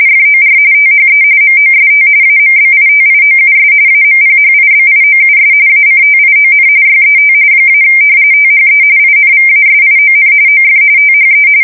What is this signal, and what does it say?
RTTY, about 45.45 baud. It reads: FOR THE MOST AMBITIOUS AMPHIBIOUS ACTION YET - THE INVASION OF THE
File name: fsk170 / battleshipnj